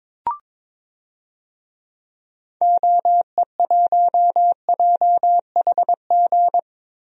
Morse code, heard as a keyed tone, about 22 wpm, 700 Hz. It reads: OE1JHG
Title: DXCC-100 Call Signs - 6 Characters - Repeated 3x - 22wpm